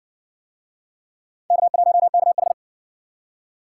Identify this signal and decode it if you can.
Morse code, heard as a keyed tone, about 57 wpm, 700 Hz. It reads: K1KH